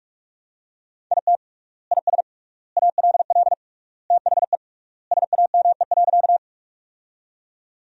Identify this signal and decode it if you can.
Morse code, heard as a keyed tone, about 46 wpm, 700 Hz. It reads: IT IS ALL THE SAME.
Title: 3x Word-Spacing - Sentences from Top 200 Words - 45wpm